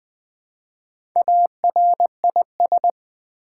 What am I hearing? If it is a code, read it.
Morse code, heard as a keyed tone, about 20 wpm, 700 Hz. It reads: ARIS
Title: Build A Word - 1 Letter At A Time - 3 to 5-Letter Words - 20wpm